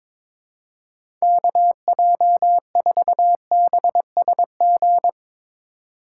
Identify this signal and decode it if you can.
Morse code, heard as a keyed tone, about 22 wpm, 700 Hz. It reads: KJ4BSG
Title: USA Call Signs - 6 Characters - 3x Character Spacing - 22wpm